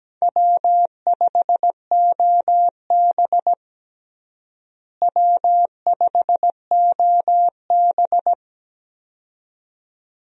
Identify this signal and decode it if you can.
Morse code, heard as a keyed tone, about 17 wpm, 700 Hz. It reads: W5OB W5OB
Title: USA Call Signs - 4 Characters - Repeated 2x - 17wpm